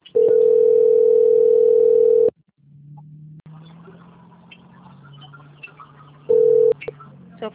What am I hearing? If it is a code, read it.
Ringback tone (North America), 440 + 480 Hz.